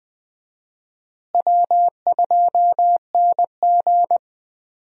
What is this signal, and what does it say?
Morse code, heard as a keyed tone, about 20 wpm, 700 Hz. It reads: W2NG
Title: USA Call Signs - 4 Characters - Repeated 3x - 20wpm